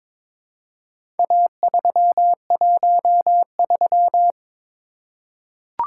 Morse code, heard as a keyed tone, about 22 wpm, 700 Hz. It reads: A313
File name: practice / 22wpm / Four Letter-Number Combinations - 22wpm